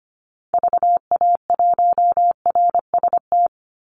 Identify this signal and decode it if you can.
Morse code, heard as a keyed tone, about 25 wpm, 700 Hz. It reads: VA1RST
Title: Call Signs Repeated 3x - 25wpm